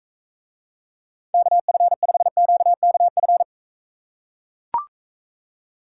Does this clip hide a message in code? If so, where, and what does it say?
Yes: Morse code from 1.3 to 3.4 s, heard as a keyed tone, about 43 wpm, 700 Hz. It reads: KF5QKF